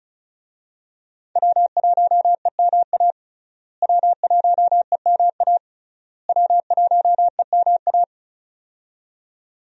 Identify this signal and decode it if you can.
Morse code, heard as a keyed tone, about 36 wpm, 700 Hz. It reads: W1EMA W1EMA W1EMA
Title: USA Call Signs - 5 Characters - Repeated 3x - 35wpm